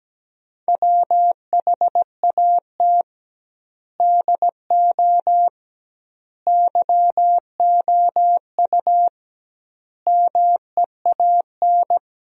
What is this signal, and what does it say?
Morse code, heard as a keyed tone, about 17 wpm, 700 Hz. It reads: WHAT DO YOU MEAN
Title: 2x Word-Spacing - Sentences from Top 400 Words - 17wpm